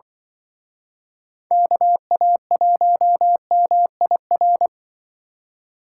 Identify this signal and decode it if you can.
Morse code, heard as a keyed tone, about 25 wpm, 700 Hz. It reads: KA1MIR
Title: Speed Racing - USA 2x3 Call Signs 20wpm